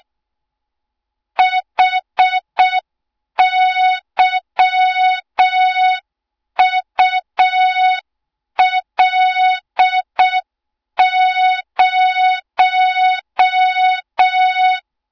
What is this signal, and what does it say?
Morse code, heard as a keyed tone, about 6 wpm, 3700 Hz. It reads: HYUL0
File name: Morze-HA2MN / hang / 20 GYAKORLAT